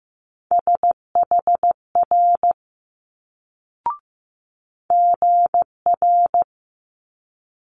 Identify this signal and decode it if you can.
Morse code, heard as a keyed tone, about 15 wpm, 700 Hz. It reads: SHR GR
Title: Multi-letter Phonemes Quick Fire 15wpm